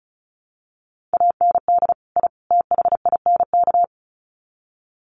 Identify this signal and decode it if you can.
Morse code, heard as a keyed tone, about 34 wpm, 700 Hz. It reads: AND I THINK
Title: Sets of 3 Words 35wpm